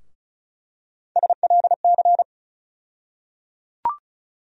Morse code, heard as a keyed tone, about 34 wpm, 700 Hz. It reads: SLC
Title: USA Call Sign Postfixes - 3 Letters 35wpm